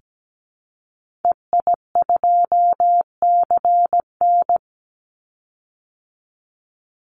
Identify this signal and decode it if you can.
Morse code, heard as a keyed tone, about 17 wpm, 700 Hz. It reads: EI2CN
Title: Call Signs Repeated 3x - 17wpm